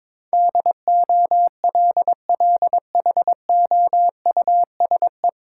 Morse code, heard as a keyed tone, about 22 wpm, 690 Hz. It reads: DOLLHOUSE
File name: Difficult Words with Hints - 22wpm